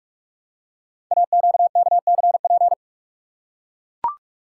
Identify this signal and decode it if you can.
Morse code, heard as a keyed tone, about 44 wpm, 700 Hz. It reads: AQKCP
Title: Code Groups - 5 Letters - 45wpm